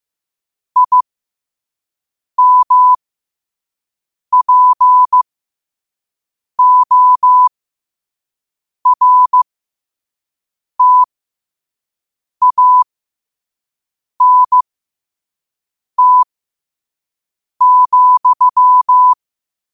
Morse code, heard as a keyed tone, about 15 wpm, spaced as for 5 wpm, 1000 Hz. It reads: IMPORTANT,